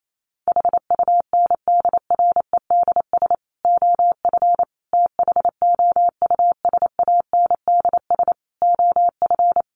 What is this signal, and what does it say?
Morse code, heard as a keyed tone, about 28 wpm, 700 Hz. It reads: HUNDREDS OF THOUSANDS OF
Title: Sets of 5 Words Encore2 28wpm